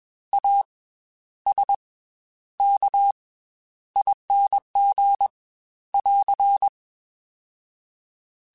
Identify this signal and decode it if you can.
Morse code, heard as a keyed tone, about 21 wpm, spaced as for 11 wpm, 800 Hz. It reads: ASKING <AR>